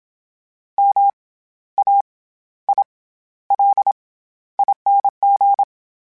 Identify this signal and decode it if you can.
Morse code, heard as a keyed tone, about 27 wpm, 800 Hz. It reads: M A I L ING
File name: Special INGs-mp3s / Special INGs-mp3s / SlowToFaster ing @25